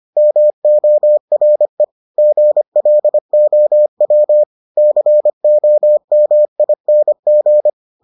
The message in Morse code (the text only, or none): MORE GLOW COMING